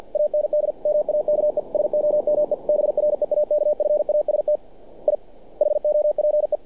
Morse code, heard as a keyed tone, about 48 wpm, 600 Hz. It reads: AND MAKE SOME BREAKFAST I HOPE
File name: archived / CRSnet